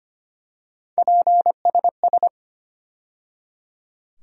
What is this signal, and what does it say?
Morse code, heard as a keyed tone, about 25 wpm, 700 Hz. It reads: PSS
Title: USA Call Sign Postfixes - 3 Letters 25wpm